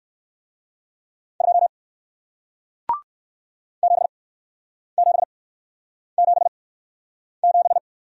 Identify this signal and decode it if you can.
Morse code, heard as a keyed tone, about 55 wpm, spaced as for 9 wpm, 700 Hz. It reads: 2 7777
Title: Speed-Racing Single Letter-Number 45wpm